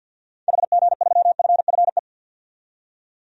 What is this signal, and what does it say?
Morse code, heard as a keyed tone, about 50 wpm, 700 Hz. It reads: HG3FFE
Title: DXCC-100 Call Signs - 6 Characters - 50wpm